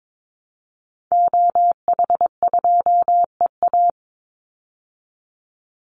Morse code, heard as a keyed tone, about 22 wpm, 700 Hz. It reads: OH2EA
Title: Call Signs Repeated 2x - 22wpm